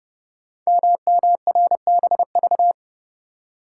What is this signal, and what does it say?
Morse code, heard as a keyed tone, about 30 wpm, 700 Hz. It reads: MMRBV
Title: Code Groups - 5 Letters - 30wpm